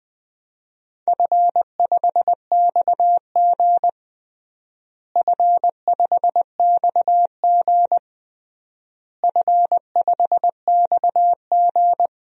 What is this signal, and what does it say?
Morse code, heard as a keyed tone, about 20 wpm, 700 Hz. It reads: F5XG F5XG F5XG
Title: DXCC-100 Call Signs - 4 Characters - Repeated 3x - 20wpm